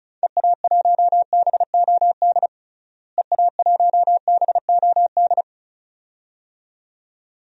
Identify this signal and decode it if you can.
Morse code, heard as a keyed tone, about 36 wpm, 700 Hz. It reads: EA1BOD EA1BOD
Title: DXCC-100 Call Signs - 6 Characters - Repeated 2x - 35wpm